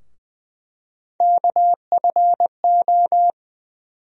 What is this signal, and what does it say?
Morse code, heard as a keyed tone, about 20 wpm, 700 Hz. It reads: KFO
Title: USA Call Sign Postfixes - 3 Letters 20wpm